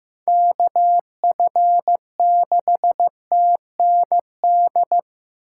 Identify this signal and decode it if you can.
Morse code, heard as a keyed tone, about 15 wpm, 700 Hz. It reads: KF6TND
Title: USA Call Signs - 6 Characters - 4x Character Spacing - 15wpm